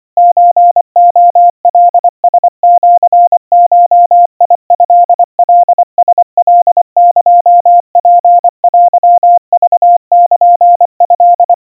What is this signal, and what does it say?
Morse code (Japanese Wabun code), heard as a keyed tone, about 24 wpm, 700 Hz. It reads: ソレカラシゴトカラカエツテクルト
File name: 90 cpm 700 Hz Crispy